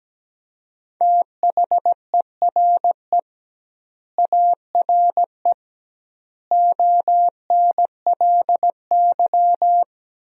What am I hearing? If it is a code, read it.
Morse code, heard as a keyed tone, about 17 wpm, 700 Hz. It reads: THERE ARE ONLY
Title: 2x Word-Spacing - Sentences from Top 300 Words - 17wpm